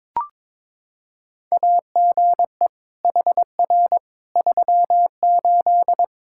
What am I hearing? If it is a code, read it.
Morse code, heard as a keyed tone, about 21 wpm, 700 Hz. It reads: AGE HR 38
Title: Practice QSO - Exchange 3 - More Info - 22wpm